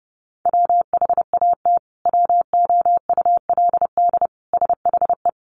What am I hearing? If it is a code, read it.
Morse code, heard as a keyed tone, about 30 wpm, 700 Hz. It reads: WHAT WOULD SHE
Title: Sentences from Top 100 Words 30wpm